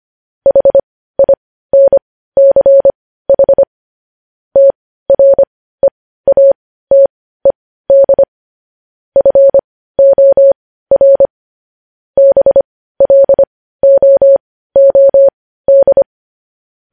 Morse code, heard as a keyed tone, about 25 wpm, spaced as for 15 wpm, 550 Hz. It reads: HINCH TREATED FOR BLOOD
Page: news headlines in morse code at 15 wpm